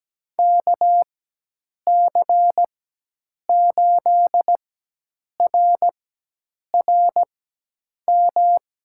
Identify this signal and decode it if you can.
Morse code, heard as a keyed tone, about 17 wpm, spaced as for 8 wpm, 700 Hz. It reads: KC8RRM
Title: USA Call Signs - 6 Characters - 4x Character Spacing - 17wpm